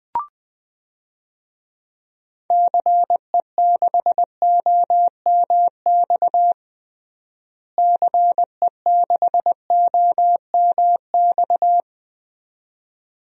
Morse code, heard as a keyed tone, about 20 wpm, 700 Hz. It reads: CE6OMX CE6OMX
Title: DXCC-100 Call Signs - 6 Characters - Repeated 2x - 20wpm